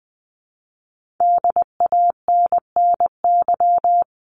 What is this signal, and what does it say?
Morse code, heard as a keyed tone, about 20 wpm, 700 Hz. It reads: DANNY
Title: CWT Exchange 20wpm